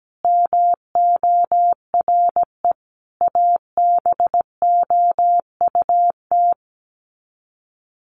Morse code, heard as a keyed tone, about 17 wpm, 700 Hz. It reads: MORE ABOUT
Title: Sets of 5 Words Encore2 17wpm